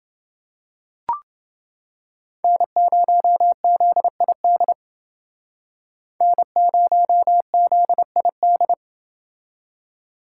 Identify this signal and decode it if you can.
Morse code, heard as a keyed tone, about 31 wpm, 700 Hz. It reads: N0ZID N0ZID